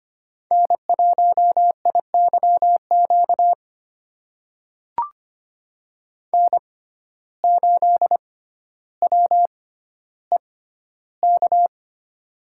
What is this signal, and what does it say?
Morse code, heard as a keyed tone, about 25 wpm, spaced as for 13 wpm, 700 Hz. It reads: N1IYQ N8WEK
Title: USA Call Signs - 5 Characters - 6x Character Spacing - 25wpm